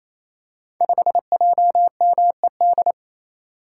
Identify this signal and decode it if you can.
Morse code, heard as a keyed tone, about 27 wpm, 700 Hz. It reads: 5JMED
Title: Code Groups - 5 Letter:Numbers - 28wpm